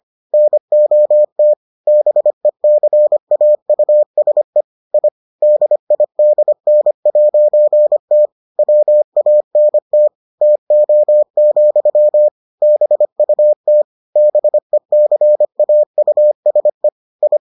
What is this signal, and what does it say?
Morse code, heard as a keyed tone, about 25 wpm, 600 Hz. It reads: NOT BECAUSE I DIDN'T WANT TO, BUT BECAUSE I